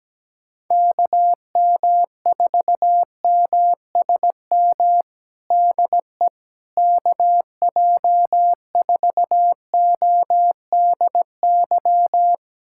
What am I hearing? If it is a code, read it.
Morse code, heard as a keyed tone, about 17 wpm, 700 Hz. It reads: KM4MSM DE KJ4ODY
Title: USA Callsign DE Callsign 17wpm